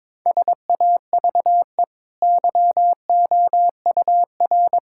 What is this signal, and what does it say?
Morse code, heard as a keyed tone, about 21 wpm, 690 Hz. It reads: SAVE YOUR
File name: practice / 22wpm / Easy Sentences with Hints - 22wpm